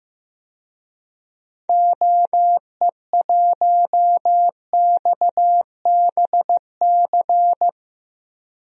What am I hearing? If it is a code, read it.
Morse code, heard as a keyed tone, about 16 wpm, 700 Hz. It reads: OE1XBC
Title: DXCC-100 Call Signs - 6 Characters - 15wpm